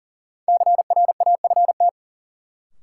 Morse code, heard as a keyed tone, about 40 wpm, 700 Hz. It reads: CRAFT
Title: Sentences from Top 1000 Words - Review Part 6 - New Word Drill - 40wpm